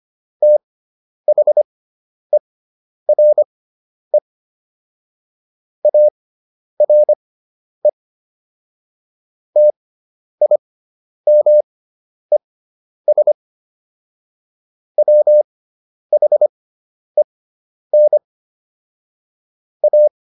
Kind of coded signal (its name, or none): Morse code